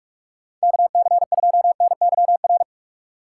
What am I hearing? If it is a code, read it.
Morse code, heard as a keyed tone, about 46 wpm, 700 Hz. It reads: KC2NYR
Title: USA Call Signs - 2x Character Spacing - 45wpm